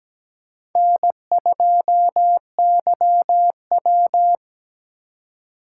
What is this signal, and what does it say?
Morse code, heard as a keyed tone, about 17 wpm, 700 Hz. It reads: N2YW
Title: USA Call Signs - 2x Character Spacing - 17wpm